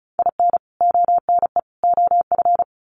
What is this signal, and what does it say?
Morse code, heard as a keyed tone, about 34 wpm, 700 Hz. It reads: IN ONE OF
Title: Sets of 3 Words 35wpm